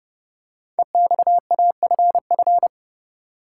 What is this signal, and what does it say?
Morse code, heard as a keyed tone, about 31 wpm, 690 Hz. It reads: EXAFF
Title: Code Groups - 5 Letters - 30wpm